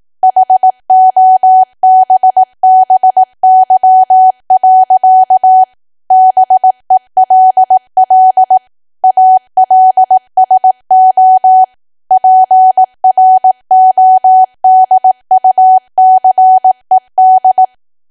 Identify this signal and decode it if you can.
Morse code, heard as a keyed tone, about 18 wpm, 750 Hz. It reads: HOBBY. BELL ALSO PRODUCED
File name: Archive / 18 WPM / 18WPM